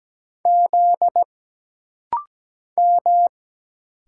Morse code, heard as a keyed tone, about 17 wpm, 700 Hz. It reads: Z M
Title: Single Letter-Number Alt - Rapid-Fire 17wpm